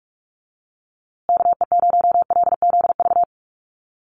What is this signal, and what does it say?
Morse code, heard as a keyed tone, about 44 wpm, 700 Hz. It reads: KE0LZV
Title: USA Call Signs Repeated 2x 45wpm